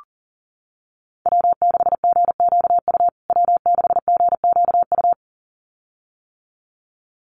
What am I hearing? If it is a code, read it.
Morse code, heard as a keyed tone, about 40 wpm, 700 Hz. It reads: W6GQU W6GQU
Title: USA Call Signs Repeated 2x 40wpm